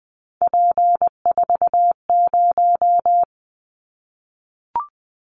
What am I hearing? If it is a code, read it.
Morse code, heard as a keyed tone, about 20 wpm, 700 Hz. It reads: P40
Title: Callsign Prefixes for Top 100 DXCC - 20wpm